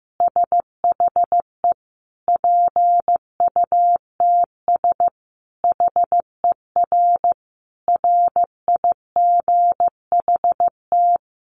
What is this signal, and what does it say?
Morse code, heard as a keyed tone, about 15 wpm, 700 Hz. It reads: SHE PUTS HER RIGHT
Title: Easy Phrases 15wpm